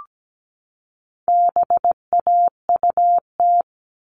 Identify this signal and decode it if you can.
Morse code, heard as a keyed tone, about 17 wpm, 700 Hz. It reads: BAUT
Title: All Cut Numbers - 4-Digit 17wpm